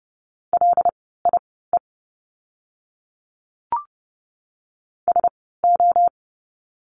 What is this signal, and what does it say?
Morse code, heard as a keyed tone, about 30 wpm, spaced as for 17 wpm, 700 Hz. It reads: LIE SO
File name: Lesson 16f - Letter c - Words 30wpm